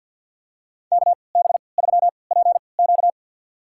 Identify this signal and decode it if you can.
Morse code, heard as a keyed tone, about 53 wpm, spaced as for 29 wpm, 700 Hz. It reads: KB3PQ
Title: USA Call Signs - 5 Characters - 3x Character Spacing - 50wpm